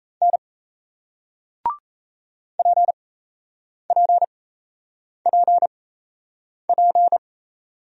Morse code, heard as a keyed tone, about 37 wpm, spaced as for 8 wpm, 700 Hz. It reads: N PPPP